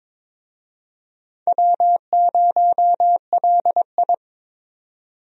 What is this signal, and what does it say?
Morse code, heard as a keyed tone, about 22 wpm, 700 Hz. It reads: W0LI
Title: USA Call Signs - 4 Characters - 22wpm